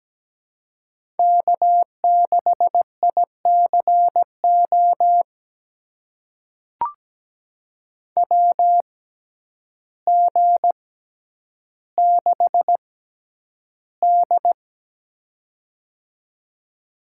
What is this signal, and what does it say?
Morse code, heard as a keyed tone, about 17 wpm, spaced as for 9 wpm, 700 Hz. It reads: K6ICO WG6D
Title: USA Call Signs - 6x Character Spacing - 17wpm